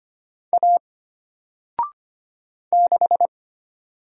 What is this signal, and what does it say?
Morse code, heard as a keyed tone, about 25 wpm, 700 Hz. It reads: A 6
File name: Single-Letter-Number-25wpm